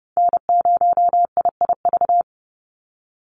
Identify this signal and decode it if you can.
Morse code, heard as a keyed tone, about 30 wpm, 700 Hz. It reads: N0IIV
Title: USA Call Signs 30wpm